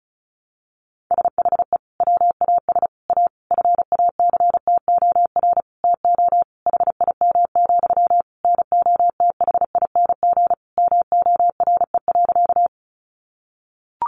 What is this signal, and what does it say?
Morse code, heard as a keyed tone, about 34 wpm, 700 Hz. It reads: SHE WAS A FACTOR TO HIM, NOTHING MORE.
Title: Sentences from Top 1300 Words 35wpm